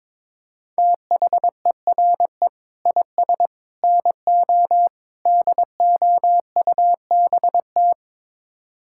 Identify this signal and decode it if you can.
Morse code, heard as a keyed tone, about 22 wpm, 700 Hz. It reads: THERE IS NO DOUBT
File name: practice / 22wpm / Build A Phrase - 5-Word Phrases - 22wpm